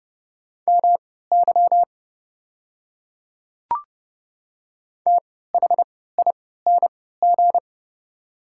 Morse code, heard as a keyed tone, about 30 wpm, spaced as for 17 wpm, 700 Hz. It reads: MY THING
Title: 3x Character Spacing - Top 100 Words - 30wpm